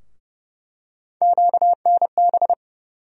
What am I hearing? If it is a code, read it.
Morse code, heard as a keyed tone, about 30 wpm, 700 Hz. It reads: QNB
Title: USA Call Sign Postfixes - 3 Letters 30wpm